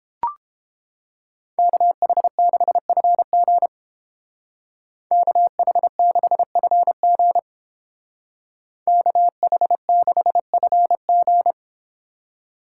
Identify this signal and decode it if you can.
Morse code, heard as a keyed tone, about 29 wpm, 700 Hz. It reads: KH6FG KH6FG KH6FG